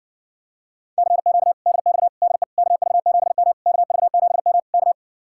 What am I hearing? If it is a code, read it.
Morse code, heard as a keyed tone, about 59 wpm, 700 Hz. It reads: CQ DX DE KV7M KV7M K